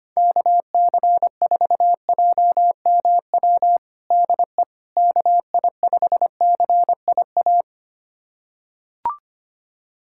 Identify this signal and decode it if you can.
Morse code, heard as a keyed tone, about 25 wpm, 700 Hz. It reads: KC4JMW DE KI5CIA
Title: USA Callsign DE Callsign 25wpm